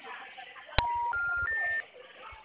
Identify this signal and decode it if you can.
Special information tone, with noise.